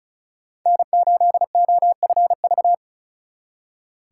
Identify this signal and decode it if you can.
Morse code, heard as a keyed tone, about 46 wpm, 700 Hz. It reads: N8OFV